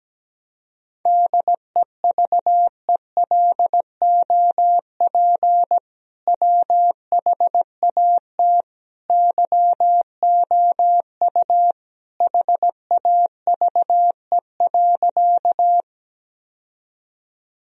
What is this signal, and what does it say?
Morse code, heard as a keyed tone, about 17 wpm, 700 Hz. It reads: DEVELOP WHAT YOU HAVE.
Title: Sentences from Top 1000 Words - Review Part 2 - 17wpm